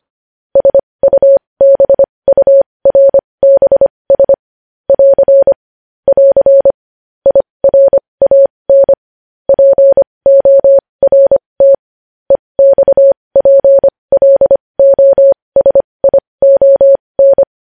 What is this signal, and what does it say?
Morse code, heard as a keyed tone, about 25 wpm, spaced as for 20 wpm, 550 Hz. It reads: SUBURBS <AR> <AR> IRAN PORT EXPLOSION
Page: news headlines in morse code at 20 wpm